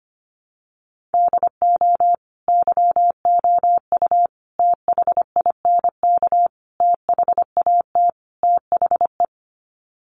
Morse code, heard as a keyed tone, about 25 wpm, 700 Hz. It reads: DO YOU THINK THAT THE
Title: Sets of 5 Words Encore 25wpm